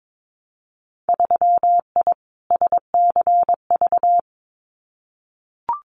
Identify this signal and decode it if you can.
Morse code, heard as a keyed tone, about 22 wpm, 700 Hz. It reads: 3I SCV
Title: Winter Field Day Exchanges Encore 22wpm